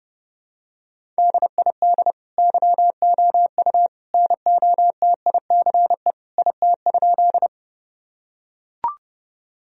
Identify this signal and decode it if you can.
Morse code, heard as a keyed tone, about 28 wpm, 700 Hz. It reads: DID YOU NOTICE IT?
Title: Sentences from Top 1000 Words - Review Part 2 - 30wpm